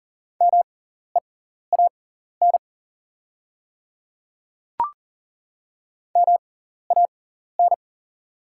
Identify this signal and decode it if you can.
Morse code, heard as a keyed tone, about 38 wpm, spaced as for 14 wpm, 700 Hz. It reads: MEAN MAN
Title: 6x Character Spacing - Top 200 Words - 40wpm